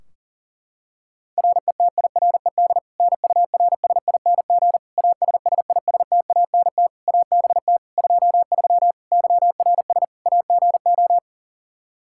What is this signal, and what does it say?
Morse code, heard as a keyed tone, about 40 wpm, 700 Hz. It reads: RETIRED NURSING ASSISTANT ABT 23 YRS AGO
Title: Practice QSO - Exchange 3 - More Info - 40wpm